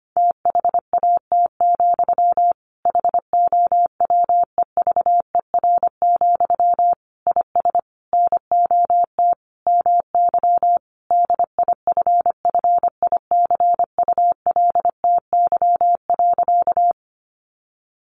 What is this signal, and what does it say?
Morse code, heard as a keyed tone, about 25 wpm, 700 Hz. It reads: THAT, HOWEVER, IS NOT MY DIFFICULTY.
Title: Sentences from Top 1500 Words 25wpm